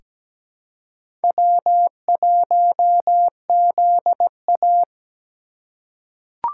Morse code, heard as a keyed tone, about 17 wpm, 700 Hz. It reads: W1ZA
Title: USA Call Signs - 3x Character Spacing - 17wpm